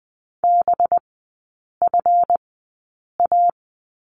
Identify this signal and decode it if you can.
Morse code, heard as a keyed tone, about 20 wpm, spaced as for 8 wpm, 700 Hz. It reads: BFA
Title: 184 Lesson 30e - Letter b - Three Characters 20wpm